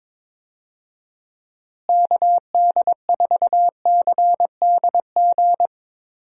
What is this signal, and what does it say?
Morse code, heard as a keyed tone, about 21 wpm, 700 Hz. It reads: KD4CDG
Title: USA Call Signs - 6 Characters - Repeated 2x - 22wpm